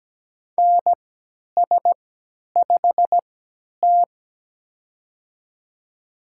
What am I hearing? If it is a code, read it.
Morse code, heard as a keyed tone, about 17 wpm, spaced as for 10 wpm, 700 Hz. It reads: NS5T
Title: USA Call Signs - 3x Character Spacing - 17wpm